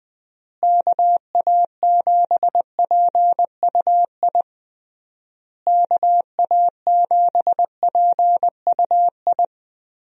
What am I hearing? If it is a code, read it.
Morse code, heard as a keyed tone, about 20 wpm, 700 Hz. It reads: KA7PUI KA7PUI
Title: USA Call Signs - 6 Characters - Repeated 3x - 20wpm